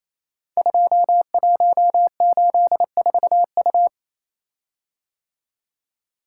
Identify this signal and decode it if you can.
Morse code, heard as a keyed tone, about 29 wpm, 700 Hz. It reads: 2184U
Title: Code Groups - 5 Letter:Numbers - 28wpm